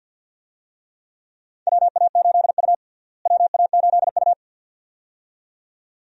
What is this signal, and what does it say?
Morse code, heard as a keyed tone, about 47 wpm, 700 Hz. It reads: WA8U WA8U
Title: USA Call Signs - 4 Characters - Repeated 2x - 50wpm